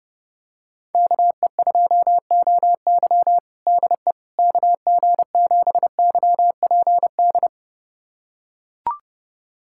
Morse code, heard as a keyed tone, about 30 wpm, 690 Hz. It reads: KE2OY DE KG7YPD
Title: USA Callsign DE Callsign 30wpm